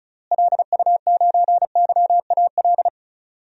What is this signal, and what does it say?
Morse code, heard as a keyed tone, about 36 wpm, 700 Hz. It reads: LU9YAL
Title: DXCC-100 Call Signs - 6 Characters - Repeated 3x - 35wpm